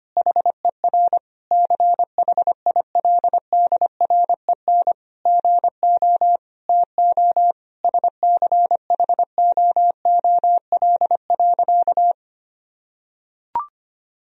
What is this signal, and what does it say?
Morse code, heard as a keyed tone, about 25 wpm, 700 Hz. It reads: HER CHILDREN GO TO SCHOOL.
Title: Sentences from Top 1000 Words - Review Part 4 - 25wpm